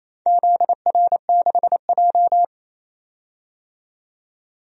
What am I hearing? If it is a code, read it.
Morse code, heard as a keyed tone, about 28 wpm, 700 Hz. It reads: ZR6J
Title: DXCC-100 Call Signs - 4 Characters - 28wpm